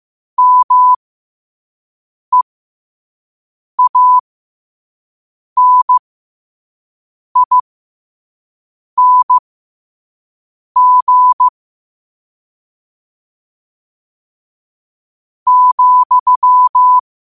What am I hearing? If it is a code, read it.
Morse code, heard as a keyed tone, about 15 wpm, spaced as for 5 wpm, 1000 Hz. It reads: MEANING ,
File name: qotdfiles / 5wpm